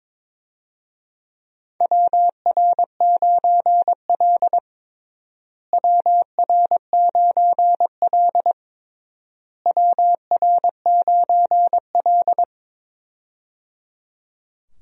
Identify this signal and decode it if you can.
Morse code, heard as a keyed tone, about 22 wpm, 700 Hz. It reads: WR9L WR9L WR9L
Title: USA Call Signs - 4 Characters - Repeated 3x - 22wpm